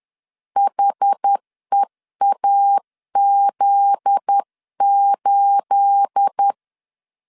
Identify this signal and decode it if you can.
Morse code, heard as a keyed tone, about 10 wpm, 800 Hz. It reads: HEAZ8
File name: LEZIONI / 9 28 10wpm 2842